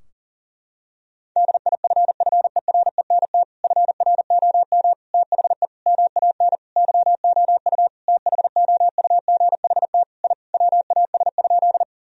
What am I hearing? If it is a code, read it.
Morse code, heard as a keyed tone, about 37 wpm, 700 Hz. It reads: DIFFERENT FROM THE MAN YOU THOUGHT I WAS?
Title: Sentences from Top 1000 Words - Review Part 3 - 40wpm